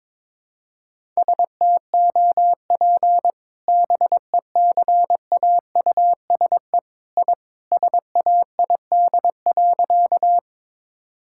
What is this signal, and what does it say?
Morse code, heard as a keyed tone, about 22 wpm, 700 Hz. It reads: STOP BECAUSE I SAID.
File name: practice / 22wpm / Sentences with 2-Word Phrases - 22wpm